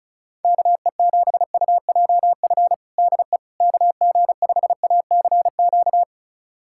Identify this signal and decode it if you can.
Morse code, heard as a keyed tone, about 34 wpm, 700 Hz. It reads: KE7UJF DE KG5ACQ
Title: USA Callsign DE Callsign - 6 Characters - 35wpm